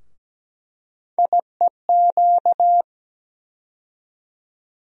Morse code, heard as a keyed tone, about 17 wpm, 700 Hz. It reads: IEQ
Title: USA Call Sign Postfixes - 3 Letters 17wpm